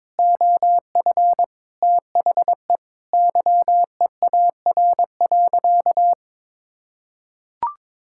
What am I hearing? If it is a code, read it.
Morse code, heard as a keyed tone, about 22 wpm, 700 Hz. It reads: OF THE YEAR.
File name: practice / 22wpm / Sentences with 3-Word Phrases - 22wpm